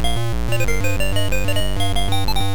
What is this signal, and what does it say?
There is no code.